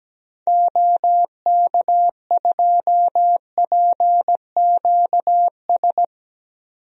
Morse code, heard as a keyed tone, about 17 wpm, 700 Hz. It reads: OK2PQS
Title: DXCC-100 Call Signs - 6 Characters - 17wpm